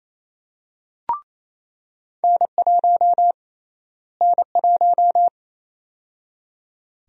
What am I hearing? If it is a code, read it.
Morse code, heard as a keyed tone, about 28 wpm, 700 Hz. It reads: N1 N1
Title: USA Call Sign Prefixes - Repeated 2x - 28wpm